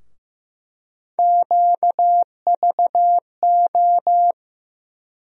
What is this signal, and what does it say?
Morse code, heard as a keyed tone, about 15 wpm, 700 Hz. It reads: QVO
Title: USA Call Sign Postfixes - 3 Letters 15wpm